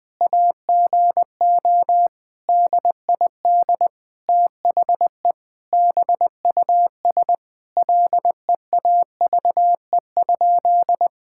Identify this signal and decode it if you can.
Morse code, heard as a keyed tone, about 20 wpm, 700 Hz. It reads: AGO DID THE BUS LEAVE?
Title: Sentences from Top 1000 Words - Review Part 6 - 20wpm